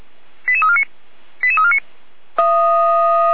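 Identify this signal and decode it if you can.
Five-tone selective call (ZVEI), heard as quick runs of five tones, tones of about 70 ms. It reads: 88349 88349
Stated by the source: ZVEI five-tone selective call received with an SDR.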